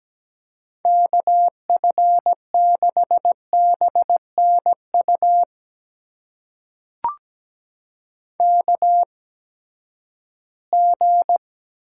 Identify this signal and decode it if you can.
Morse code, heard as a keyed tone, about 17 wpm, spaced as for 12 wpm, 700 Hz. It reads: KF6BNU KG
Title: USA Call Signs - 6 Characters - 8x Character Spacing - 17wpm